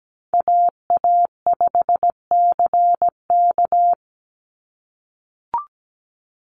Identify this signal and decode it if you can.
Morse code, heard as a keyed tone, about 17 wpm, 700 Hz. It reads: AA5CK
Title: NA SOTA Chaser Callsigns 17wpm